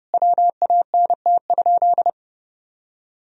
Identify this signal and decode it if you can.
Morse code, heard as a keyed tone, about 30 wpm, 700 Hz. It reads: WANT?
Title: Sentences from Top 200 Words and Abbreviations 30wpm